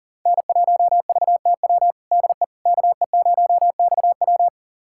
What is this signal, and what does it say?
Morse code, heard as a keyed tone, about 40 wpm, 700 Hz. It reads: N1VTW DE KE0XW
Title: USA Callsign DE Callsign - 5 Characters - 40wpm